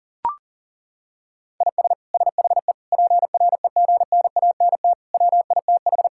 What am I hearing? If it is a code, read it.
Morse code, heard as a keyed tone, about 37 wpm, 700 Hz. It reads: IS SHE PREGNANT WITH
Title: Sentences from Top 1000 Words - Review Part 6 - 40wpm